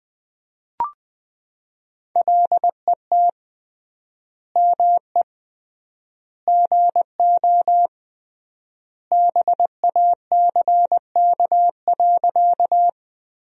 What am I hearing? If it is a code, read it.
Morse code, heard as a keyed tone, about 21 wpm, 700 Hz. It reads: LET ME GO BACK.
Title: 3x Word-Spacing - Sentences from Top 300 Words - 20wpm